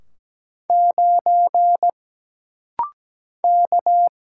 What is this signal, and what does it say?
Morse code, heard as a keyed tone, about 17 wpm, 700 Hz. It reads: 9 K
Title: Single Letter-Number - Rapid-Fire 17wpm